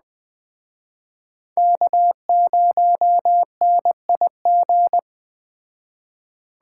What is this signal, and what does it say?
Morse code, heard as a keyed tone, about 21 wpm, 700 Hz. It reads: K0NIG
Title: Speed Racing - USA 1x3 Call Signs 17wpm